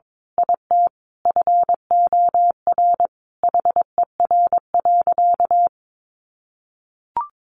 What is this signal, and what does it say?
Morse code, heard as a keyed tone, about 22 wpm, 700 Hz. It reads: IT FOR HER.
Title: Sentences from Top 1400 Words 22wpm